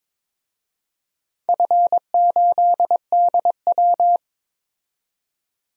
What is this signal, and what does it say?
Morse code, heard as a keyed tone, about 22 wpm, 700 Hz. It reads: F8DW
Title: DXCC-100 Call Signs - 4 Characters - 22wpm